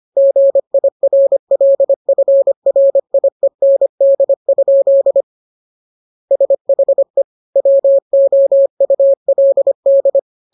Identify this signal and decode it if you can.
Morse code, heard as a keyed tone, about 25 wpm, 550 Hz. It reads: GIRLFRIEND? SHE WOULD